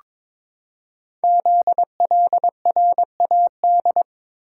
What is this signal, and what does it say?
Morse code, heard as a keyed tone, about 22 wpm, 700 Hz. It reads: ZLRAD
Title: Code Groups - 5 Letter:Numbers - 22wpm